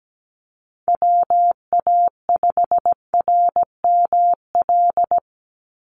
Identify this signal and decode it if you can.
Morse code, heard as a keyed tone, about 17 wpm, 700 Hz. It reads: WA5RML
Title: Call Signs - 17wpm